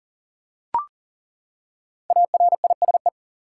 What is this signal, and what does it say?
Morse code, heard as a keyed tone, about 40 wpm, 700 Hz. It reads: ARISE
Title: Five Letter Words - 40wpm